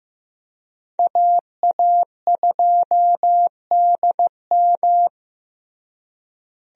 Morse code, heard as a keyed tone, about 15 wpm, 700 Hz. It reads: AA2DM
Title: USA Call Signs - 5 Characters - Repeated 2x - 15wpm